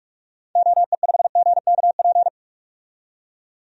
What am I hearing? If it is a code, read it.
Morse code, heard as a keyed tone, about 46 wpm, 700 Hz. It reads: OE5GKP
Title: DXCC-100 Call Signs - 6 Characters - 45wpm